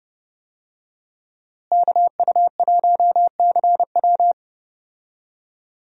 Morse code, heard as a keyed tone, about 30 wpm, 700 Hz. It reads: KU1CW
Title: USA Call Signs - 5 Characters - Repeated 3x - 30wpm